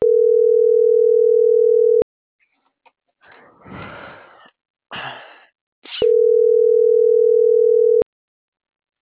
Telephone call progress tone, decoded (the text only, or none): ringback tone